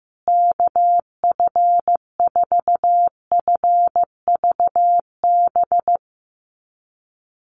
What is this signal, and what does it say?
Morse code, heard as a keyed tone, about 15 wpm, 700 Hz. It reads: KF4FVB
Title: USA Call Signs Repeated 3x 15wpm